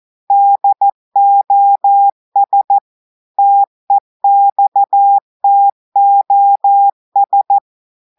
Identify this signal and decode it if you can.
Morse code, heard as a keyed tone, about 14 wpm, 800 Hz. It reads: DOS TEXTOS
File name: audios / portugais / portugais texte en clair 1 vitesse 14 mots minutes